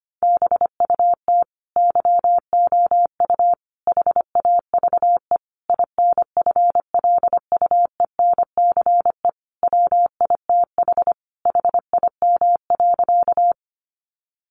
Morse code, heard as a keyed tone, about 25 wpm, 700 Hz. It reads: BUT YOU HAVE INFLUENCE WITH HIM.
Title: Sentences from Top 1100 Words 25wpm